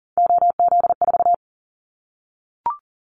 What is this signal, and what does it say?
Morse code, heard as a keyed tone, about 40 wpm, 700 Hz. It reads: OZ4
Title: Callsign Prefixes for Top 50 DXCC - 40wpm